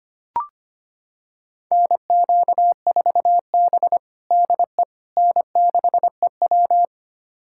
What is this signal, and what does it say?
Morse code, heard as a keyed tone, about 25 wpm, 700 Hz. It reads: NQ4B DE N6EW
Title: USA Callsign DE Callsign - 4 Characters - 25wpm